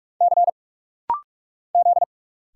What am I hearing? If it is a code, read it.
Morse code, heard as a keyed tone, about 46 wpm, 700 Hz. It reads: C Z
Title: Single Letters - Rapid-Fire 45wpm